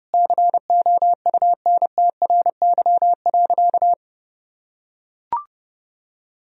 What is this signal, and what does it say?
Morse code, heard as a keyed tone, about 30 wpm, 700 Hz. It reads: COUNTRY.
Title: Sentences from Top 300 Words and Abbreviations 30wpm